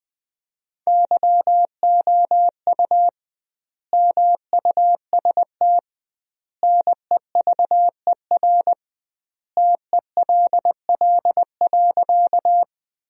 Morse code, heard as a keyed tone, about 20 wpm, 700 Hz. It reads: YOU MUST NEVER TELL.
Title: 2x Word-Spacing - Sentences from Top 300 Words - 20wpm